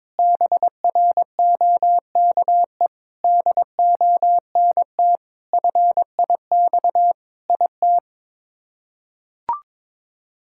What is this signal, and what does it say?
Morse code, heard as a keyed tone, about 21 wpm, 700 Hz. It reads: BROKE DONT FIX IT
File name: practice / 22wpm / Sets of 7 Words - 22wpm